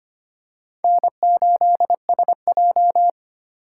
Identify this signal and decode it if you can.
Morse code, heard as a keyed tone, about 25 wpm, 700 Hz. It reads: N8SJ
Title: USA Call Signs - 4 Characters - 2x Character Spacing - 25wpm